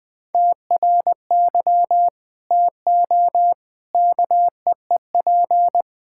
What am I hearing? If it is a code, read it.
Morse code, heard as a keyed tone, about 20 wpm, 700 Hz. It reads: TRY TO KEEP
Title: Sentences with 2-Word Phrases - 20wpm